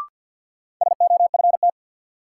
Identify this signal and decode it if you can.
Morse code, heard as a keyed tone, about 53 wpm, 700 Hz. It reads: SOFT
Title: Sentences from Top 1500 Words - New Word Drill - 50wpm